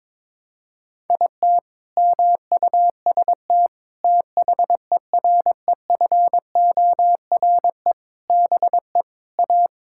Morse code, heard as a keyed tone, about 22 wpm, 700 Hz. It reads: IT MUST THEREFORE BE A
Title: Sentences from Top 1000 Words - Review Part 6 - 22wpm